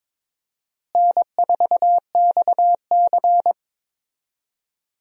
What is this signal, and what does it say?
Morse code, heard as a keyed tone, about 22 wpm, 700 Hz. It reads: N4XC
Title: USA Call Signs - 4 Characters - Repeated 3x - 22wpm